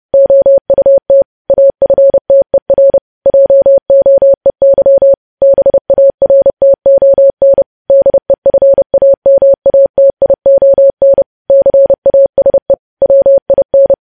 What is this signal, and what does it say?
Morse code, heard as a keyed tone, about 30 wpm, 550 Hz. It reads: OUT AFTER JOEY BARTON DEFAMATION CASE WIN
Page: news headlines in morse code at 30 wpm